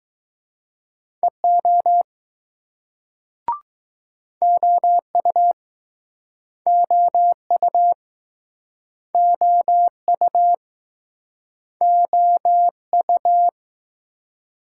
Morse code, heard as a keyed tone, about 21 wpm, 700 Hz. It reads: EO OU OU OU OU